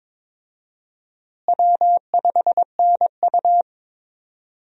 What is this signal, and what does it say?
Morse code, heard as a keyed tone, about 22 wpm, 700 Hz. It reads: W5NU
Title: USA Call Signs - 4 Characters - Repeated 2x - 22wpm